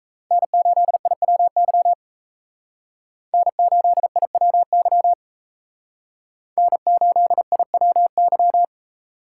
Morse code, heard as a keyed tone, about 36 wpm, 700 Hz. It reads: N8IWY N8IWY N8IWY